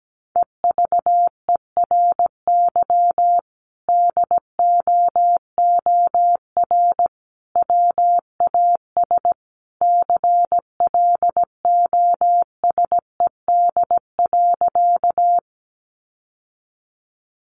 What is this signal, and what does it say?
Morse code, heard as a keyed tone, about 17 wpm, 700 Hz. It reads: EVERY DOOR WAS CLOSED.
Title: Sentences from Top 1400 Words 17wpm